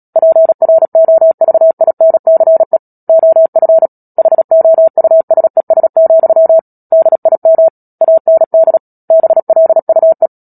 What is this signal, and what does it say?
Morse code, heard as a keyed tone, about 37 wpm, 650 Hz. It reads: PROVINCE OF HOUSES, DIM AND BLUE
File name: Book Two / 10-Chapter 10 - Epilogue - 36 WPM